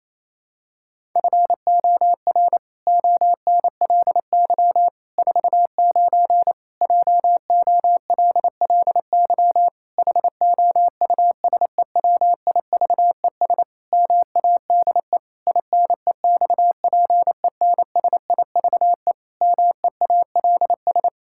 Morse code, heard as a keyed tone, about 28 wpm, 700 Hz. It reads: FOR ONLY 49 JOLLY HOUSEWIVES MADE INEXPENSIVE MEALS
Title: Pangram Challenge v2 - 28wpm